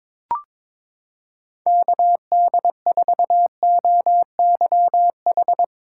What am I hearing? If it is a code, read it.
Morse code, heard as a keyed tone, about 22 wpm, 700 Hz. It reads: KD4OYH
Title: USA Callsign DE Callsign - 6 Characters - 22wpm